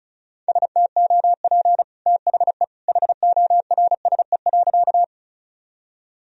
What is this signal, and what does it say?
Morse code, heard as a keyed tone, about 36 wpm, 700 Hz. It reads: STOP THE HORSE.
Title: Sentences from Top 1000 Words - Review Part 1 - 35wpm